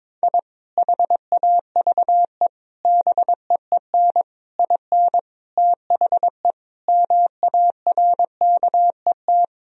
Morse code, heard as a keyed tone, about 22 wpm, 700 Hz. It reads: I HAVE BEEN IN THE MARKET
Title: Sentences from Top 1000 Words - Review Part 3 - 22wpm